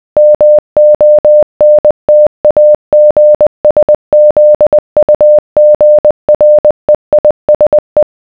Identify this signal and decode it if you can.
Morse code, heard as a keyed tone, about 20 wpm, 600 Hz. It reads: MONTAGSZUGREISE